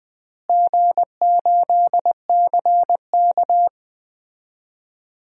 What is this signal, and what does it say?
Morse code, heard as a keyed tone, about 21 wpm, 700 Hz. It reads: G8CK
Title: DXCC-100 Call Signs - 4 Characters - Repeated 3x - 20wpm